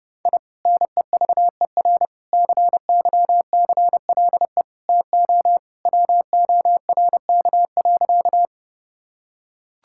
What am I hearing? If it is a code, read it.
Morse code, heard as a keyed tone, about 30 wpm, 700 Hz. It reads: I NEVER CYCLE TO WORK.
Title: Sentences from Top 1000 Words - Review Part 6 - 30wpm